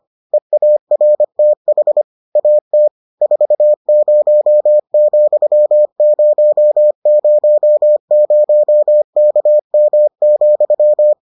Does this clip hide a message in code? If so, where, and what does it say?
Yes: Morse code from 0.3 to 11.1 s, heard as a keyed tone, about 25 wpm, 600 Hz. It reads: EARTH AT 40,000KM,